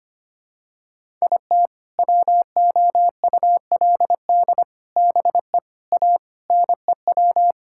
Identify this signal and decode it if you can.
Morse code, heard as a keyed tone, about 25 wpm, 700 Hz. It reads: IT WOULD BE A NEW
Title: Sentences from Top 1000 Words - Review Part 5 - 25wpm